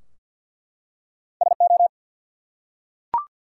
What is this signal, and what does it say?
Morse code, heard as a keyed tone, about 50 wpm, 700 Hz. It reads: SO
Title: Two Letter Words - 50wpm